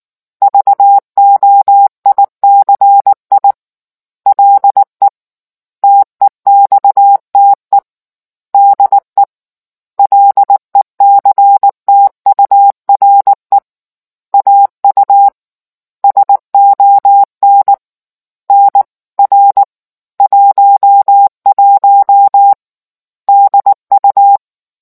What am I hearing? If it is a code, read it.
Morse code, heard as a keyed tone, about 19 wpm, 800 Hz. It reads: VOICI LE TEXTE DE LECTURE AU SON NR 11 DU